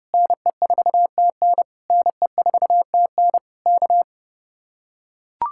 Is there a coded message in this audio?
Yes: Morse code from 0.1 to 4.0 s, heard as a keyed tone, about 30 wpm, 700 Hz. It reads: NE4TN NE4TN K